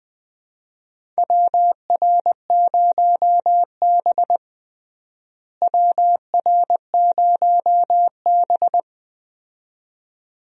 Morse code, heard as a keyed tone, about 20 wpm, 700 Hz. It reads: WR0B WR0B
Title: USA Call Signs - 4 Characters - Repeated 2x - 20wpm